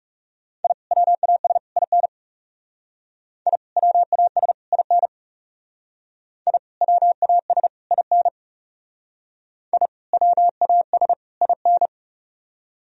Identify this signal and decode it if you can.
Morse code, heard as a keyed tone, about 39 wpm, 700 Hz. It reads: I WAS IN I WAS IN I WAS IN I WAS IN